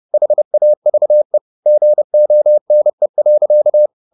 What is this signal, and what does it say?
Morse code, heard as a keyed tone, about 30 wpm, 600 Hz. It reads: HAVE GONE.
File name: the old man and the sea / the old man and the sea 1 30wpm